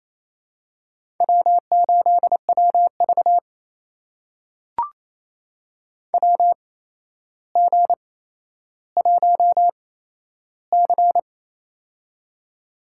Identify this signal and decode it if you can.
Morse code, heard as a keyed tone, about 28 wpm, spaced as for 12 wpm, 700 Hz. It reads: W8WV WG1C
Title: USA Call Signs - 4 Characters - 8x Character Spacing - 28wpm